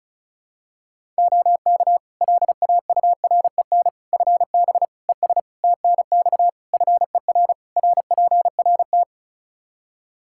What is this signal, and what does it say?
Morse code, heard as a keyed tone, about 34 wpm, 700 Hz. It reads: OK LAUREN FB ES TNX FER RPRT
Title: Practice QSO - Exchange 2 - Info - 35wpm